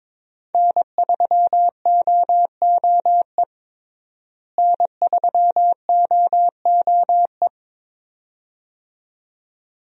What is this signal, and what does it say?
Morse code, heard as a keyed tone, about 21 wpm, 700 Hz. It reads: N3OOE N3OOE
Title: USA Call Signs - 5 Characters - Repeated 2x - 22wpm